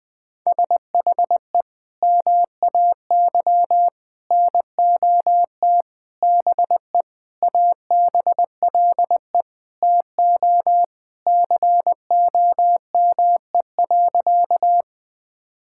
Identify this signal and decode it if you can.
Morse code, heard as a keyed tone, about 20 wpm, 700 Hz. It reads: SHE MAY NOT BE ABLE TO COME.
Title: Sentences with 5-Word Phrases - Spoken Beginning - 20wpm